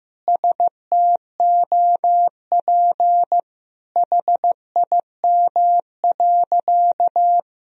Morse code, heard as a keyed tone, about 15 wpm, 700 Hz. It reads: STOP HIM.
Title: Sentences from Top 1000 Words - Review Part 4 - 15wpm